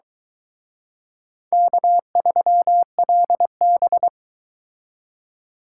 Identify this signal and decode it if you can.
Morse code, heard as a keyed tone, about 23 wpm, 700 Hz. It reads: K3LB